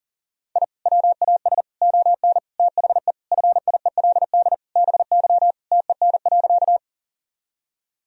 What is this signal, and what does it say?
Morse code, heard as a keyed tone, about 40 wpm, 700 Hz. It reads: I WAS ON THE FIELD BY TEN.
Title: Sentences from Top 1000 Words - Review Part 2 - 40wpm